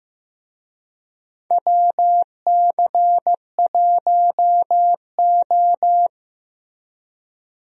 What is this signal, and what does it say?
Morse code, heard as a keyed tone, about 15 wpm, 700 Hz. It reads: WC1O
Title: USA Call Signs - 4 Characters - Repeated 2x - 15wpm